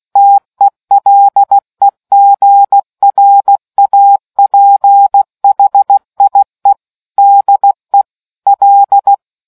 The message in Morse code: TELEGRAPHIE DE L